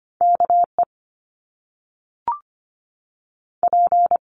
Morse code, heard as a keyed tone, about 25 wpm, 700 Hz. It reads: KE P
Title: Top Two Letter Combinations in Words 25wpm